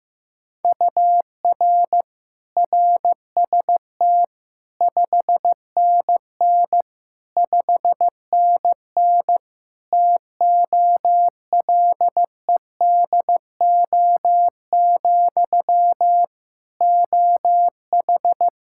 Morse code, heard as a keyed tone, about 15 wpm, 700 Hz. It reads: UR RST 5NN 5NN TOLEDO, OH
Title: Practice QSO - Exchange 1 - Beginning - 15wpm